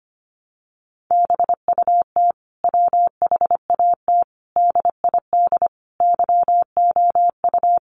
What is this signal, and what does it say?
Morse code, heard as a keyed tone, about 25 wpm, 700 Hz. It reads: BUT WHAT DID YOU
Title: Sentences from Top 300 Words 25wpm